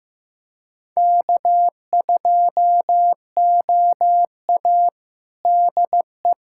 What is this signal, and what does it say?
Morse code, heard as a keyed tone, about 15 wpm, 700 Hz. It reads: K2OA DE
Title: USA Callsign DE Callsign - 4 Characters - 15wpm